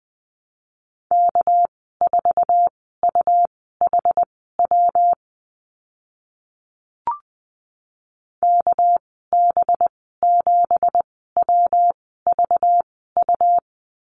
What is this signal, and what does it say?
Morse code, heard as a keyed tone, about 20 wpm, spaced as for 14 wpm, 700 Hz. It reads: K4UHW KB7WVU
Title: 231 Lesson 37g - Letter x - Call Signs 20wpm